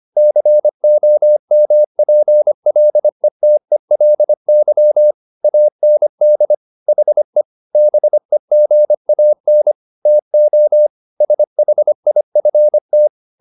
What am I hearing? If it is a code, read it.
Morse code, heard as a keyed tone, about 25 wpm, 600 Hz. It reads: COMPLETELY AND HE BEGAN TO SHIFT
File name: the old man and the sea part 1 25wpm